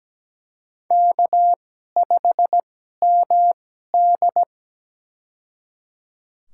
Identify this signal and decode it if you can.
Morse code, heard as a keyed tone, about 17 wpm, spaced as for 12 wpm, 700 Hz. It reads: K5MD
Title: USA Call Signs - 4 Characters - 2x Character Spacing - 17wpm